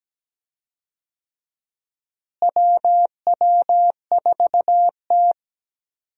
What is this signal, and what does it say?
Morse code, heard as a keyed tone, about 17 wpm, 700 Hz. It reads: WW4T
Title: USA Call Signs - 4 Characters - Repeated 3x - 17wpm